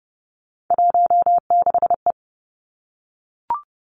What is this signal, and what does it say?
Morse code, heard as a keyed tone, about 30 wpm, 700 Hz. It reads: 16E
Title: Three Challenging Letter-Number Combinations 30wpm